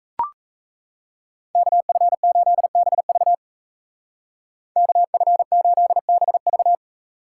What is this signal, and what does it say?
Morse code, heard as a keyed tone, about 33 wpm, 700 Hz. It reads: KF8BV KF8BV
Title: Speed Racing - USA 2x2 Call Signs 28wpm